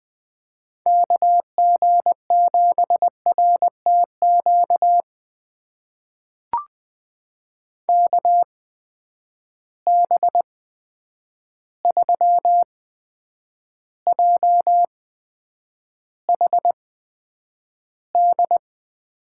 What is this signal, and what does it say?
Morse code, heard as a keyed tone, about 20 wpm, spaced as for 9 wpm, 700 Hz. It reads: KG7RTQ KB3JHD
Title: USA Call Signs - 6 Characters - 8x Character Spacing - 20wpm